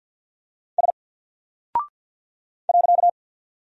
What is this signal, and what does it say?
Morse code, heard as a keyed tone, about 50 wpm, 700 Hz. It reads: S .
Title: Single Character (Letter:Number:Punctuation:Prosign) 50wpm